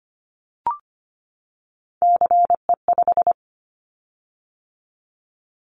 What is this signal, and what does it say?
Morse code, heard as a keyed tone, about 25 wpm, 700 Hz. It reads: CE5
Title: Callsign Prefixes for Top 100 DXCC - 25wpm